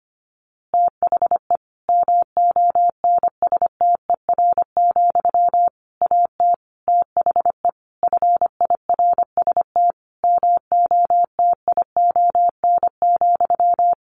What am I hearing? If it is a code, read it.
Morse code, heard as a keyed tone, about 25 wpm, 700 Hz. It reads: THE MONSTER, AT THE FIRST MOTION,
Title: A Sound Of Thunder 25wpm